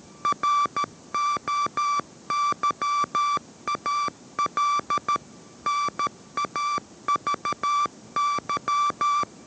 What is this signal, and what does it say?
Morse code, heard as a keyed tone, about 13 wpm, 1160 Hz. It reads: ROYAL NAVY